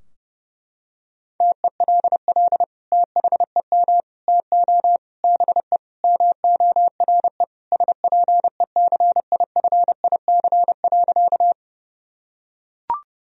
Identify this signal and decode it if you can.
Morse code, heard as a keyed tone, about 30 wpm, 700 Hz. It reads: TELL THEM TO BE MORE SPECIFIC.
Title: Sentences from Top 1000 Words - Review Part 3 - 30wpm